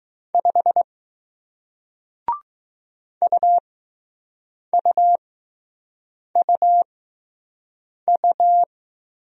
Morse code, heard as a keyed tone, about 20 wpm, spaced as for 6 wpm, 700 Hz. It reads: 5 UUUU